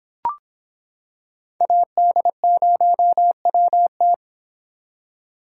Morse code, heard as a keyed tone, about 25 wpm, 700 Hz. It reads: AD0WT